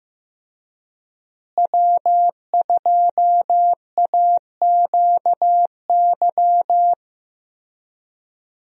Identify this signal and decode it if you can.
Morse code, heard as a keyed tone, about 15 wpm, 700 Hz. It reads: W2AQY
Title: USA Call Signs - 5 Characters - Repeated 3x - 15wpm